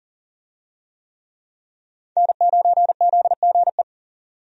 Morse code, heard as a keyed tone, about 40 wpm, 700 Hz. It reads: N9ZGE
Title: USA Call Signs - 5 Characters - 40wpm